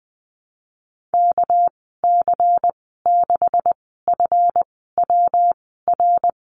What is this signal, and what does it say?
Morse code, heard as a keyed tone, about 20 wpm, spaced as for 15 wpm, 700 Hz. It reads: KC6FWR
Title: 207 Lesson 33g - Letter j - Call Signs 20wpm